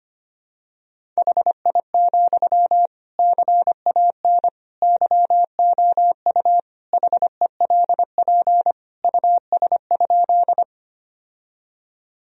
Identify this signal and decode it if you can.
Morse code, heard as a keyed tone, about 25 wpm, 700 Hz. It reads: HI, CAN YOU HELP US?
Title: Sentences from Top 1000 Words - Review Part 3 - 25wpm